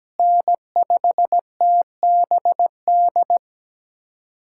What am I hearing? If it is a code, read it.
Morse code, heard as a keyed tone, about 17 wpm, 700 Hz. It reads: N5TBD
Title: USA Callsign DE Callsign - 5 Characters - 17wpm